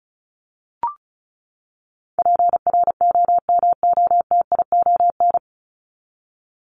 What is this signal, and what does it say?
Morse code, heard as a keyed tone, about 34 wpm, 700 Hz. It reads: PROMOTION
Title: Top 1501-1750 Words 35wpm